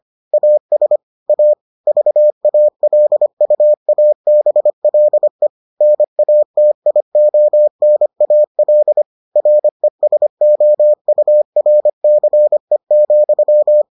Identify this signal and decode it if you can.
Morse code, heard as a keyed tone, about 25 wpm, 600 Hz. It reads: AS A VALUABLE NATIONAL RESOURCE,